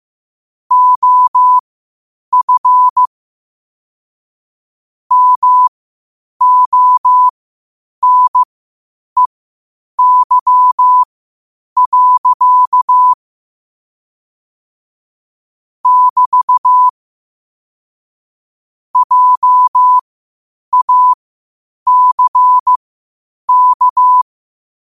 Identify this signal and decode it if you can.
Morse code, heard as a keyed tone, about 15 wpm, spaced as for 8 wpm, 1000 Hz. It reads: OF MONEY. <BT> JACK
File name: qotdfiles / 8wpm